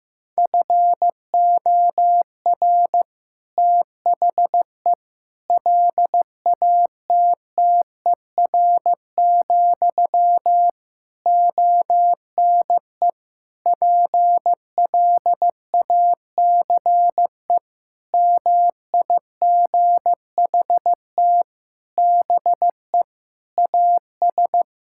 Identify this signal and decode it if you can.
Morse code, heard as a keyed tone, about 15 wpm, 700 Hz. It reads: FOR THE LATTER, ONE PLACE MIGHT BE AS
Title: Sentences from Top 1600 Words 15wpm